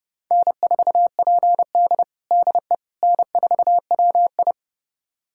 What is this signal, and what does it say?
Morse code, heard as a keyed tone, about 30 wpm, 700 Hz. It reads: N4PD DE N4WI
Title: USA Callsign DE Callsign - 4 Characters - 30wpm